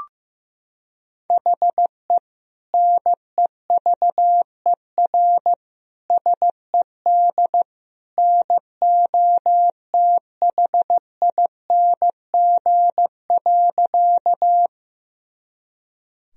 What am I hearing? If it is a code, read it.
Morse code, heard as a keyed tone, about 16 wpm, 700 Hz. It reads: HE NEVER SED NOTHING.
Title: Sentences from Top 500 Words and Abbreviations 15wpm